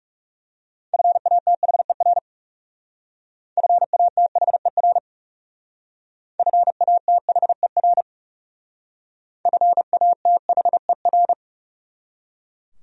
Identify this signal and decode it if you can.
Morse code, heard as a keyed tone, about 37 wpm, 700 Hz. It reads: FATHER FATHER FATHER FATHER
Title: Speed-Racing Top 500 Words 30wpm